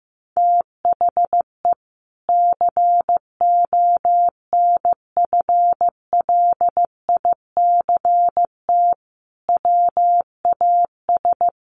Morse code, heard as a keyed tone, about 15 wpm, 700 Hz. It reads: THE CONFLICT WAS
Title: Sentences from Top 1300 Words 15wpm